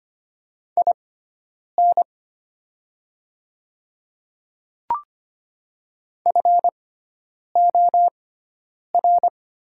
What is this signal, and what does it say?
Morse code, heard as a keyed tone, about 25 wpm, spaced as for 9 wpm, 700 Hz. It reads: IN FOR
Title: 6x Character Spacing - Top 200 Words - 25wpm